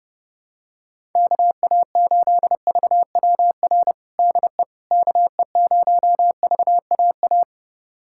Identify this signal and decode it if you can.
Morse code, heard as a keyed tone, about 32 wpm, 700 Hz. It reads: KA8VWR DE KE0VAA
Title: USA Callsign DE Callsign - 6 Characters - 30wpm